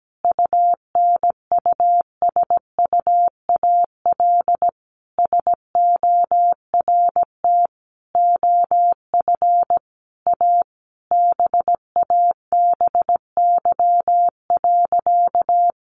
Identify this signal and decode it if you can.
Morse code, heard as a keyed tone, about 17 wpm, 700 Hz. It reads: UNUSUAL SORT OF A BABY.
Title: Sentences from Top 1400 Words 17wpm